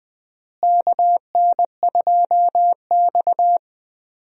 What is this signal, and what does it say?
Morse code, heard as a keyed tone, about 20 wpm, 700 Hz. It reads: KN2X
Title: USA Call Signs - 4 Characters - Repeated 3x - 20wpm